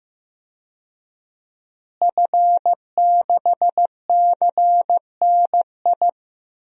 Morse code, heard as a keyed tone, about 15 wpm, 700 Hz. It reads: F6CNI
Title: DXCC-100 Call Signs - 5 Characters - Repeated 2x - 15wpm